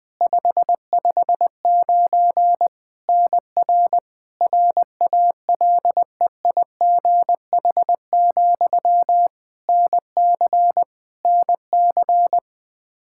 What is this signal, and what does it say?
Morse code, heard as a keyed tone, about 20 wpm, 700 Hz. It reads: 559 NR RALEIGH, NC NC
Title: K1USN SST QSO Practice 20wpm